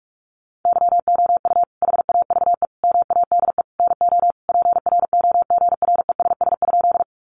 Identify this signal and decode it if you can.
Morse code, heard as a keyed tone, about 44 wpm, 700 Hz. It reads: YOU HAVE MADE NO PROGRESS?
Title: Sentences from Top 1400 Words 45wpm